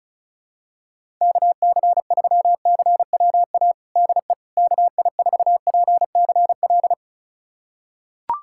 Morse code, heard as a keyed tone, about 34 wpm, 700 Hz. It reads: KC3CWA DE KI4PCL
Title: USA Callsign DE Callsign 35wpm